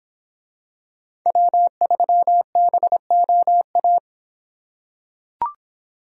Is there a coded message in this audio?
Yes: Morse code from 1.3 to 4.0 s, heard as a keyed tone, about 25 wpm, 700 Hz. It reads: W3BOA